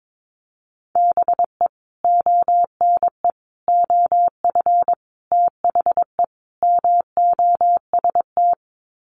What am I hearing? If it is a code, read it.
Morse code, heard as a keyed tone, about 22 wpm, 700 Hz. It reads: BE ONE OF THE MOST
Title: Sets of 5 Words Encore 22wpm